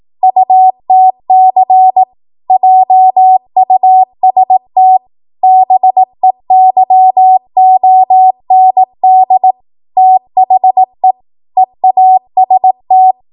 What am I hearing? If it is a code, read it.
Morse code, heard as a keyed tone, about 18 wpm, 750 Hz. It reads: UTC JUST BEYOND THE EAST
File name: Morse / 18WPM